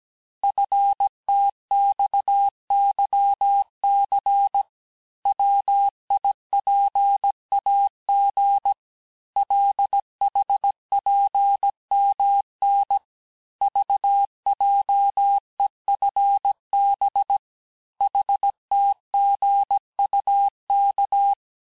Morse code, heard as a keyed tone, about 17 wpm, 800 Hz. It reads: FTXYC WIPAG LHPMN VJEFB HTGUK
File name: cours-du-lundi files / corrige-cours-1581-texte-chiffre-vitesse-16-mpm